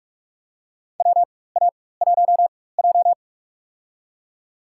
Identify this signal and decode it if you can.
Morse code, heard as a keyed tone, about 46 wpm, spaced as for 21 wpm, 700 Hz. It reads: WA1J
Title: USA Call Signs - 4 Characters - 4x Character Spacing - 45wpm